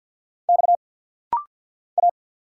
Morse code, heard as a keyed tone, about 50 wpm, 700 Hz. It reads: X A
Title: Single Letter-Number Alt - Rapid-Fire 50wpm